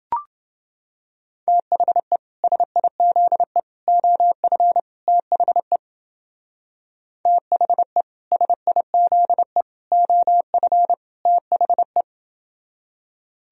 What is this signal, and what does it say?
Morse code, heard as a keyed tone, about 29 wpm, 700 Hz. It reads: THE SIZE OF THE THE SIZE OF THE